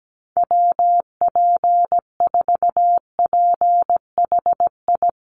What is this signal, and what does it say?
Morse code, heard as a keyed tone, about 17 wpm, 700 Hz. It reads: WP4PHI
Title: USA Call Signs Repeated 3x 17wpm